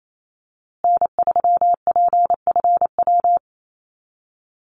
Morse code, heard as a keyed tone, about 28 wpm, 700 Hz. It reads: N3PFW
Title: USA Call Signs Repeated 3x 28wpm